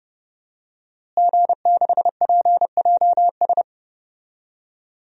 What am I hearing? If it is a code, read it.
Morse code, heard as a keyed tone, about 30 wpm, 700 Hz. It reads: G6PJS
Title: DXCC-100 Call Signs - 5 Characters - Repeated 2x - 30wpm